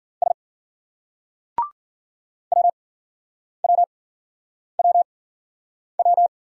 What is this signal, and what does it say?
Morse code, heard as a keyed tone, about 50 wpm, spaced as for 9 wpm, 700 Hz. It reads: S WWWW